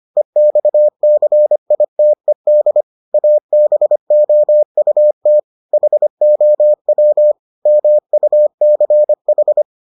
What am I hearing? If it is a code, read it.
Morse code, heard as a keyed tone, about 25 wpm, 600 Hz. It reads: EXCITED ABOUT HOW MUCH